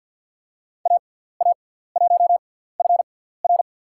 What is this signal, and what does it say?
Morse code, heard as a keyed tone, about 50 wpm, spaced as for 17 wpm, 700 Hz. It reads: AA1FR
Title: USA Call Signs - 5 Characters - 6x Character Spacing - 50wpm